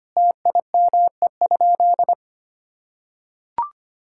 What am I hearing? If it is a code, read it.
Morse code, heard as a keyed tone, about 25 wpm, 700 Hz. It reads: TIME?
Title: Sentences from Top 1000 Words - Review Part 6 - 25wpm